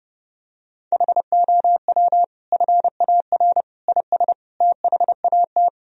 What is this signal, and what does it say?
Morse code, heard as a keyed tone, about 30 wpm, 700 Hz. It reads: HOW FAR IS THAT
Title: Sentences from Top 1000 Words - Review Part 1 - 30wpm